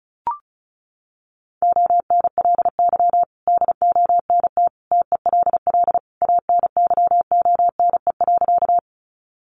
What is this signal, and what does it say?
Morse code, heard as a keyed tone, about 34 wpm, 700 Hz. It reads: ONLY DONT TELL ANYONE.
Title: Sentences from Top 900 Words 35wpm